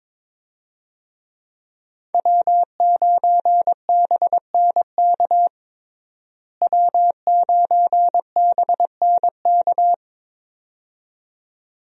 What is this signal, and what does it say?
Morse code, heard as a keyed tone, about 22 wpm, 700 Hz. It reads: W9BNK W9BNK
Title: USA Call Signs - 5 Characters - Repeated 2x - 22wpm